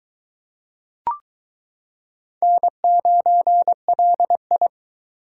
Morse code, heard as a keyed tone, about 22 wpm, 700 Hz. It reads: N9LI